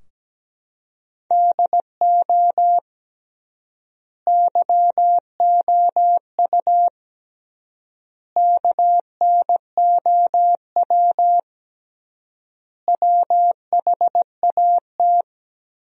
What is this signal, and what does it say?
Morse code, heard as a keyed tone, about 17 wpm, 700 Hz. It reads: DO YOU KNOW WHAT
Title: 3x Word-Spacing - Sentences from Top 100 Words - 17wpm